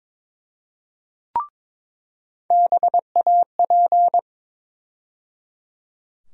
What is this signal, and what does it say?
Morse code, heard as a keyed tone, about 21 wpm, 700 Hz. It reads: BAP
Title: USA Call Sign Postfixes - 3 Letters 22wpm